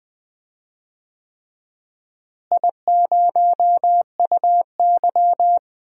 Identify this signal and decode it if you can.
Morse code, heard as a keyed tone, about 20 wpm, 700 Hz. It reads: I0UY
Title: DXCC-100 Call Signs - 4 Characters - Repeated 3x - 20wpm